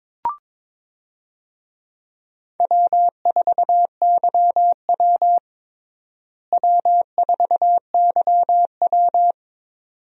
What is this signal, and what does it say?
Morse code, heard as a keyed tone, about 22 wpm, 700 Hz. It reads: W4YW W4YW
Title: USA Call Signs - 4 Characters - Repeated 2x - 22wpm